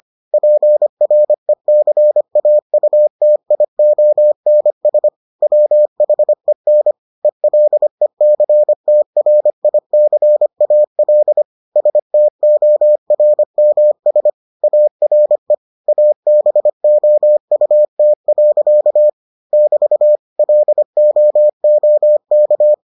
Morse code, heard as a keyed tone, about 25 wpm, 600 Hz. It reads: PRECAUTIONS WHEN ELECTRICAL STORMS ARE ABOUT. <BT> LOOK